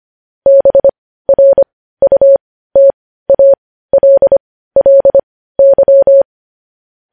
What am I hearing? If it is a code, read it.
Morse code, heard as a keyed tone, about 25 wpm, spaced as for 15 wpm, 550 Hz. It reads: BRUTALLY
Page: news-headlines-in-morse-code-at-15-wpm